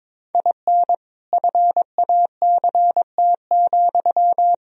Morse code, heard as a keyed tone, about 21 wpm, 700 Hz. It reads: IN FACT,
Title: Sentences from Top 1000 Words - Review Part 3 - 22wpm